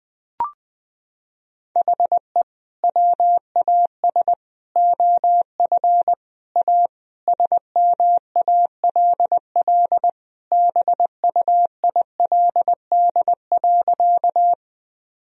Morse code, heard as a keyed tone, about 20 wpm, 700 Hz. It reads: HE WAS OF A SMALL BUILD.
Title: Sentences from Top 1000 Words - Review Part 1 - 20wpm